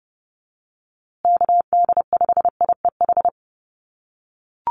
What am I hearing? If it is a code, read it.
Morse code, heard as a keyed tone, about 30 wpm, 700 Hz. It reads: KD5IEH
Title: USA Call Signs Repeated 2x 30wpm